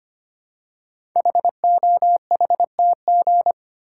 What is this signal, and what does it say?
Morse code, heard as a keyed tone, about 25 wpm, 700 Hz. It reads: HOHTG
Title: Code Groups - 5 Letters - 25wpm